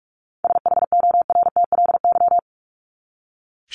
Morse code, heard as a keyed tone, about 44 wpm, 700 Hz. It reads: SHORTLY
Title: LY Suffix - Top 100 Words with Base Words 45wpm